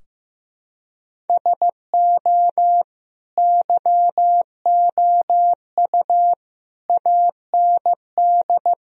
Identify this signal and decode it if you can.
Morse code, heard as a keyed tone, about 15 wpm, 700 Hz. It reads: SO YOU AND
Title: Sentences from Top 1000 Words - Review Part 4 - 15wpm